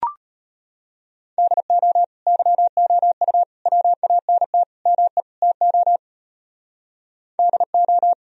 Morse code, heard as a keyed tone, about 30 wpm, 700 Hz. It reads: DO YOU WANT ME TO DO
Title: Speed-Racing - Sets of 5 Words 25wpm